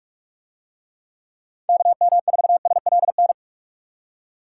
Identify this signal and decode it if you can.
Morse code, heard as a keyed tone, about 45 wpm, 700 Hz. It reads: KM4SLN